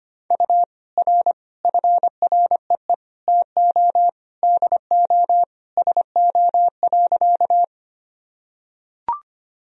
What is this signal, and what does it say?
Morse code, heard as a keyed tone, about 25 wpm, 700 Hz. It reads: U R FREE TO DO SO.
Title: Sentences from Top 500 Words and Abbreviations 25wpm